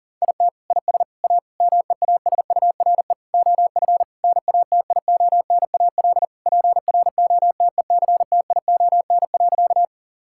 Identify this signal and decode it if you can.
Morse code, heard as a keyed tone, about 37 wpm, 700 Hz. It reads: IT IS A MEASURE OF NATIONAL PROTECTION.
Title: Sentences from Top 1600 Words 40wpm